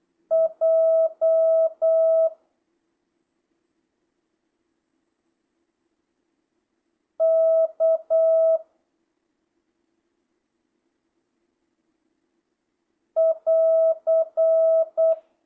Morse code, heard as a keyed tone, about 8 wpm, 1300 Hz. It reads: J K <AR>